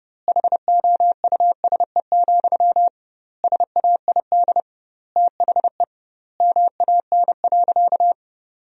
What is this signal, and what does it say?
Morse code, heard as a keyed tone, about 30 wpm, 700 Hz. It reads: HOUSE, SAID THE MAN.
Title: 2x Word-Spacing - Sentences from Top 400 Words - 30wpm